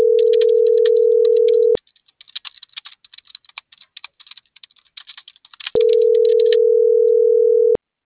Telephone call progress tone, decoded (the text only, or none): ringback tone